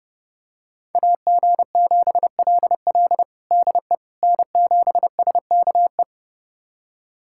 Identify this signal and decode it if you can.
Morse code, heard as a keyed tone, about 30 wpm, 700 Hz. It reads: AG7LL DE N7SKE
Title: USA Callsign DE Callsign - 5 Characters - 30wpm